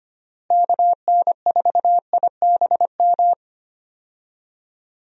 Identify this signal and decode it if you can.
Morse code, heard as a keyed tone, about 25 wpm, 700 Hz. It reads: KN4IBM
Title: USA Call Signs - 6 Characters - Repeated 2x - 25wpm